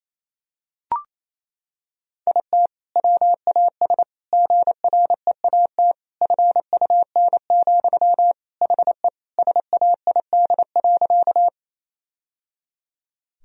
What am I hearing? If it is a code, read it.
Morse code, heard as a keyed tone, about 28 wpm, 700 Hz. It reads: IT WAS GREAT FUN, HE SAID.
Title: Sentences from Top 1000 Words - Review Part 4 - 28wpm